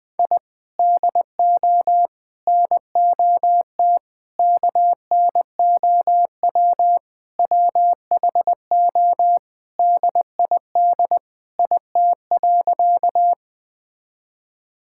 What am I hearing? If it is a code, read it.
Morse code, heard as a keyed tone, about 20 wpm, 700 Hz. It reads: I DO NOT KNOW WHO DID IT.
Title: Sentences with 5-Word Phrases - 20wpm